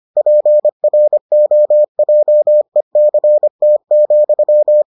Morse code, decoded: PROJECT,

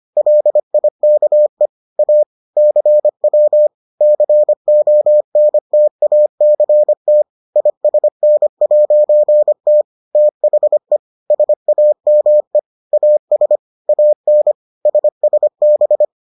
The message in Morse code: LIKE A CW CONTACT ISN'T THE SAME AS AN SSB

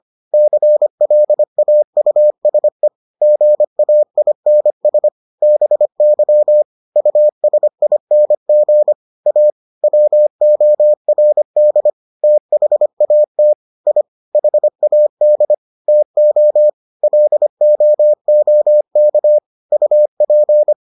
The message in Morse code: CLAUSE GAINS BY USING A WORD THAT I HAD TO LOOK UP

